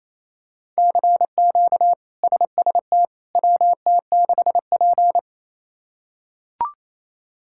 Morse code, heard as a keyed tone, about 27 wpm, 700 Hz. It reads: CQ SST WT6P
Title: K1USN SST QSO Practice - 28wpm